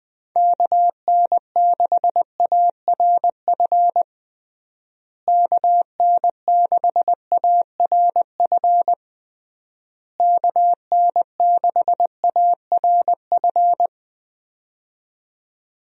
Morse code, heard as a keyed tone, about 20 wpm, 700 Hz. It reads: KN6ARF KN6ARF KN6ARF
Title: USA Call Signs - 6 Characters - Repeated 3x - 20wpm